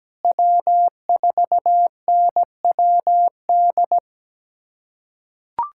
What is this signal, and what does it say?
Morse code, heard as a keyed tone, about 17 wpm, 700 Hz. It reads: W4NWD
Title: USA Callsign DE Callsign - 5 Characters - 17wpm